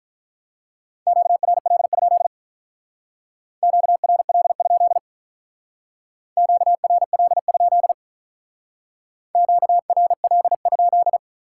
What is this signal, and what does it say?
Morse code, heard as a keyed tone, about 42 wpm, 700 Hz. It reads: QRL? QRL? QRL? QRL?
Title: Speed-Racing Top 90 QSO Elements 35wpm